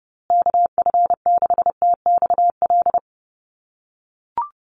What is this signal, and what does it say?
Morse code, heard as a keyed tone, about 30 wpm, 700 Hz. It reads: KF6TXL
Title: USA Call Signs Repeated 3x 30wpm